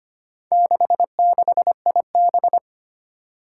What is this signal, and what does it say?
Morse code, heard as a keyed tone, about 25 wpm, 700 Hz. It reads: 66IB
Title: Four Challenging Letter-Number Combinations 25wpm